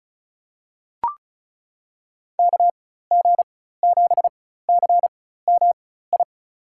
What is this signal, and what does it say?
Morse code, heard as a keyed tone, about 36 wpm, spaced as for 16 wpm, 700 Hz. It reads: KG7CMI
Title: USA Call Signs - 6 Characters - 4x Character Spacing - 35wpm